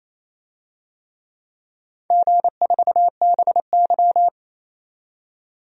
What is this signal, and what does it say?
Morse code, heard as a keyed tone, about 28 wpm, 700 Hz. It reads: G4BY
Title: DXCC-100 Call Signs - 4 Characters - 28wpm